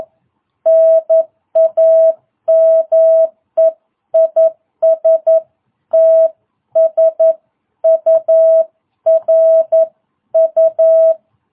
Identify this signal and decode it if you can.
Morse code, heard as a keyed tone, about 11 wpm, 660 Hz. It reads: NAME IS TSURU